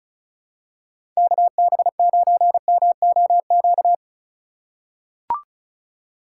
Morse code, heard as a keyed tone, about 36 wpm, 700 Hz. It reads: KB9MOQ
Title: USA Call Signs - 6 Characters - 3x Character Spacing - 35wpm